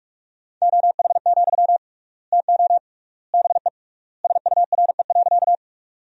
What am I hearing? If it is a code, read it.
Morse code, heard as a keyed tone, about 41 wpm, 700 Hz. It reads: OH, TO BE SURE.
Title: 3x Word-Spacing - Sentences from Top 500 Words - 45wpm